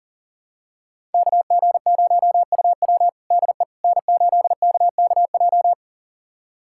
Morse code, heard as a keyed tone, about 40 wpm, 700 Hz. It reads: KG0UW DE N8KKJ
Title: USA Callsign DE Callsign - 5 Characters - 40wpm